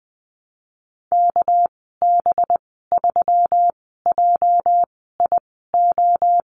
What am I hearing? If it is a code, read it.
Morse code, heard as a keyed tone, about 20 wpm, spaced as for 14 wpm, 700 Hz. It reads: KB3JIO
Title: 245 Lesson 39g - Letter z - Call Signs 20wpm